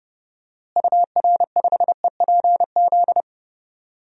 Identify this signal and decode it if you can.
Morse code, heard as a keyed tone, about 32 wpm, 700 Hz. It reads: UR5EPZ
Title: DXCC-100 Call Signs - 6 Characters - Repeated 2x - 30wpm